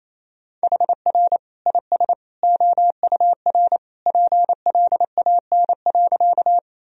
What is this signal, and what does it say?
Morse code, heard as a keyed tone, about 28 wpm, 700 Hz. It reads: HR IS OUR PLAN.
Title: Sentences from Top 500 Words and Abbreviations 28wpm